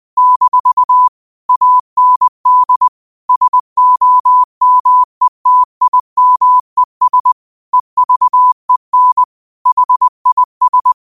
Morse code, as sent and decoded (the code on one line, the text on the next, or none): -....- / .- -. -.. / ... --- -- . - .. -- . ... / . ...- . -. / .... .. ...
- AND SOMETIMES EVEN HIS